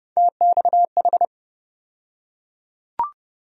Morse code, heard as a keyed tone, about 30 wpm, 700 Hz. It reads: TXH
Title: USA Call Sign Postfixes - 3 Letters 30wpm